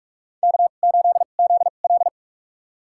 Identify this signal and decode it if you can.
Morse code, heard as a keyed tone, about 46 wpm, spaced as for 33 wpm, 700 Hz. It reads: K8ZL
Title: USA Call Signs - 4 Characters - 2x Character Spacing - 45wpm